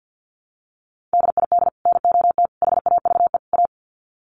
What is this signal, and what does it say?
Morse code, heard as a keyed tone, about 50 wpm, 700 Hz. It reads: DID NOT HAVE A
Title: Sets of 4 Words 50wpm